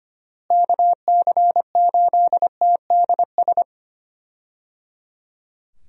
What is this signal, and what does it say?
Morse code, heard as a keyed tone, about 25 wpm, 700 Hz. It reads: KC8TDS
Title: USA Call Signs - 6 Characters - Repeated 3x - 25wpm